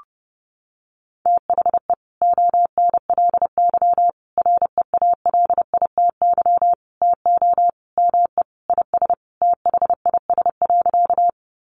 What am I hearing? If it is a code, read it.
Morse code, heard as a keyed tone, about 30 wpm, 700 Hz. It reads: THE ONLY REALITY TO ME IS THIS.
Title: Sentences from Top 1200 Words 30wpm